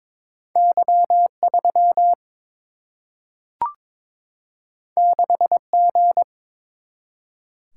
Morse code, heard as a keyed tone, about 22 wpm, 700 Hz. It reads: Y 3 6 G
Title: Single-Letter Single-Number 22wpm